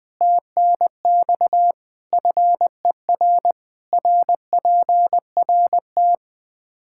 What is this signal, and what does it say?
Morse code, heard as a keyed tone, about 21 wpm, 700 Hz. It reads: TNX FER RPRT
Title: Practice QSO - Exchange 2 - Info - 20wpm